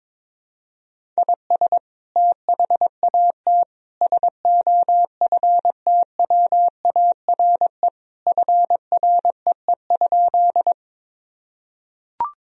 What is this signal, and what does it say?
Morse code, heard as a keyed tone, about 22 wpm, 700 Hz. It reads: IS THAT SOFTWARE FREE?
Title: Sentences from Top 1000 Words - Review Part 3 - 22wpm